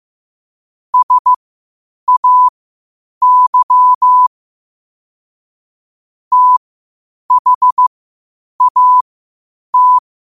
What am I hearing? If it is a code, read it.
Morse code, heard as a keyed tone, about 15 wpm, spaced as for 8 wpm, 1000 Hz. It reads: SAY THAT
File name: qotdfiles / 8wpm